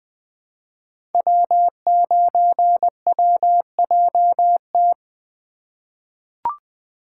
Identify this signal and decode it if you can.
Morse code, heard as a keyed tone, about 20 wpm, 700 Hz. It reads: W9WJT
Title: Code Groups - 5 Letter:Numbers - 20wpm